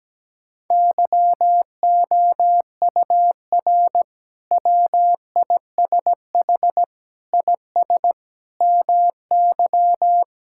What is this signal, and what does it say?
Morse code, heard as a keyed tone, about 17 wpm, 700 Hz. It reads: YOUR WISH IS MY
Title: Sentences from Top 1600 Words 17wpm